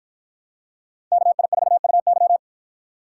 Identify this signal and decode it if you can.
Morse code, heard as a keyed tone, about 50 wpm, 700 Hz. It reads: KI4UY